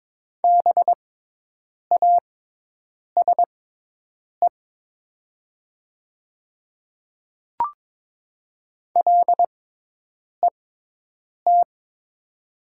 Morse code, heard as a keyed tone, about 22 wpm, spaced as for 8 wpm, 700 Hz. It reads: BASE LET
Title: 6x Character Spacing - Top 500 Words - 22wpm